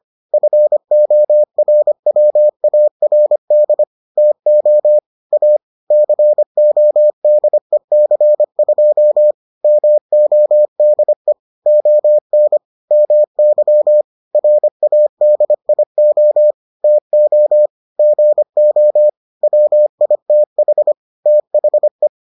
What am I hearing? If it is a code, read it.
Morse code, heard as a keyed tone, about 25 wpm, 600 Hz. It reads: FORWARD TO A CODEC2 MODE ON MY RADIO TO GO WITH THE